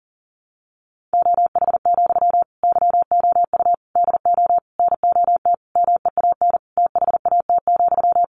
Morse code, heard as a keyed tone, about 40 wpm, 700 Hz. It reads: OH, YOU DO NOT MEAN THAT,
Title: Sentences from Top 600 Words 40wpm